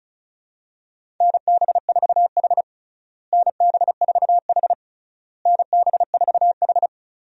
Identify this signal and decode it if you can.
Morse code, heard as a keyed tone, about 34 wpm, 700 Hz. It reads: NB4H NB4H NB4H
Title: USA Call Signs - 4 Characters - Repeated 3x - 35wpm